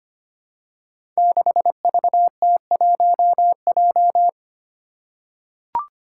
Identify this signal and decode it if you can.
Morse code, heard as a keyed tone, about 25 wpm, 700 Hz. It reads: 6VT1J
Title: Five Challenging Letter-Number Combinations 25wpm